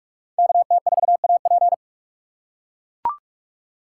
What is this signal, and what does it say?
Morse code, heard as a keyed tone, about 44 wpm, 700 Hz. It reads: KT4AP